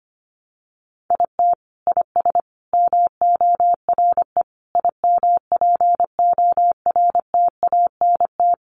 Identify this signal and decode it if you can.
Morse code, heard as a keyed tone, about 25 wpm, 700 Hz. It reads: IT IS MORE IMPORTANT
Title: Sentences from Top 600 Words 25wpm